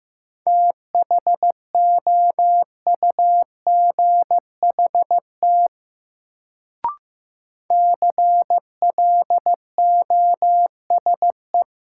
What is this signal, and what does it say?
Morse code, heard as a keyed tone, about 15 wpm, 700 Hz. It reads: THOUGHT CLOSE
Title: Sentences from Top 300 Words - New Word Drill - 15wpm